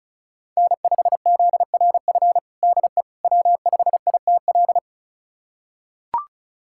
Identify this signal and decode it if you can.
Morse code, heard as a keyed tone, about 34 wpm, 700 Hz. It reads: N5ZRF DE W5ITL
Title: USA Callsign DE Callsign - 5 Characters - 35wpm